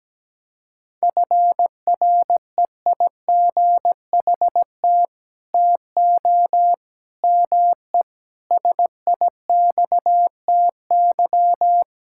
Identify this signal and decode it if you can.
Morse code, heard as a keyed tone, about 17 wpm, 700 Hz. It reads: FREIGHT TO ME SIXTY
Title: Pangram Challenge v2 - 17wpm